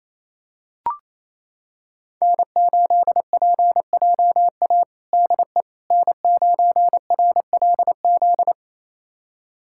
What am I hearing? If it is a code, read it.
Morse code, heard as a keyed tone, about 28 wpm, 700 Hz. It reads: N8PJA DE N9RLZ
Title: USA Callsign DE Callsign - 5 Characters - 28wpm